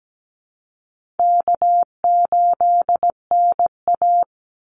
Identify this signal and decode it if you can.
Morse code, heard as a keyed tone, about 17 wpm, 700 Hz. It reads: K8NA
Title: Easy Call Signs - 17wpm